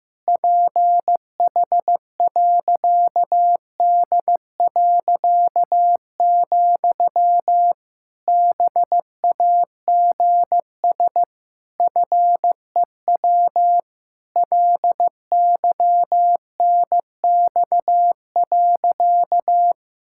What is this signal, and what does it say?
Morse code, heard as a keyed tone, about 15 wpm, 700 Hz. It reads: PH.D., BAGS FEW LYNX.
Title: Pangram Challenge v2 - 15wpm